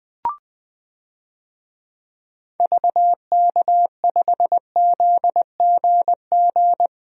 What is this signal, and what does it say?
Morse code, heard as a keyed tone, about 21 wpm, 700 Hz. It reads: VK5ZGG
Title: DXCC-100 Call Signs - 6 Characters - 20wpm